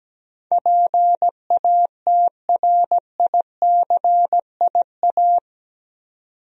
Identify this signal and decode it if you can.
Morse code, heard as a keyed tone, about 17 wpm, 700 Hz. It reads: PATRICIA
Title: USA Common Names - 17wpm